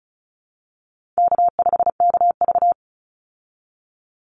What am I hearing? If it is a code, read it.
Morse code, heard as a keyed tone, about 34 wpm, 700 Hz. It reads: K5KV
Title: Easy Call Signs Repeated 2x - 35wpm